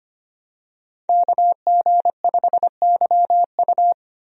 Morse code, heard as a keyed tone, about 25 wpm, 700 Hz. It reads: KG5YU
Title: USA Call Signs - 5 Characters - Repeated 2x - 25wpm